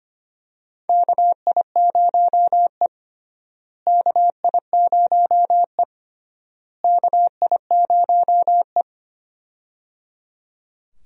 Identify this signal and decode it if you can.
Morse code, heard as a keyed tone, about 25 wpm, 700 Hz. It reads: KI0E KI0E KI0E
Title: USA Call Signs - 4 Characters - Repeated 3x - 25wpm